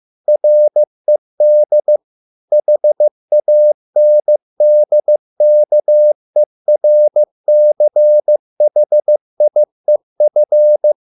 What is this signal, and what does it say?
Morse code, heard as a keyed tone, about 15 wpm, 600 Hz. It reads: RED HANDKERCHIEF